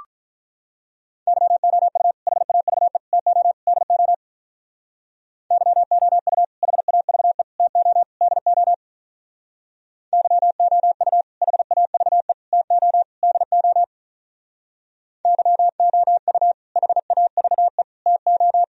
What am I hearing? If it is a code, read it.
Morse code, heard as a keyed tone, about 44 wpm, 700 Hz. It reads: YOU HAVE TO DO YOU HAVE TO DO YOU HAVE TO DO YOU HAVE TO